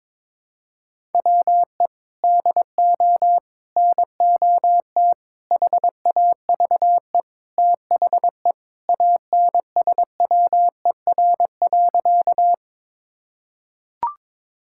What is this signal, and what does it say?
Morse code, heard as a keyed tone, about 22 wpm, 700 Hz. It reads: WE DO NOT HAVE THE ANSWER.
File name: practice / 22wpm / Sentences with 4-Word Phrases - 22wpm